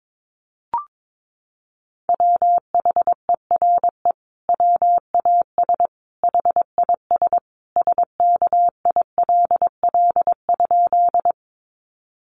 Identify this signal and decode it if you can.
Morse code, heard as a keyed tone, about 22 wpm, 700 Hz. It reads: WHERE WAS HIS SKILL?
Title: Sentences from Top 1200 Words 22wpm